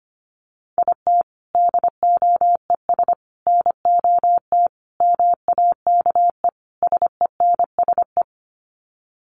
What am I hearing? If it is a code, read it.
Morse code, heard as a keyed tone, about 25 wpm, 700 Hz. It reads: IT DOES NOT MAKE SENSE
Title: Sets of 5 Words Encore2 25wpm